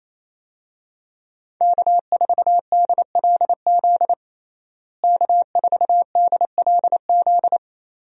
Morse code, heard as a keyed tone, about 28 wpm, 700 Hz. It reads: K4DLZ K4DLZ
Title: USA Call Signs - 5 Characters - Repeated 3x - 28wpm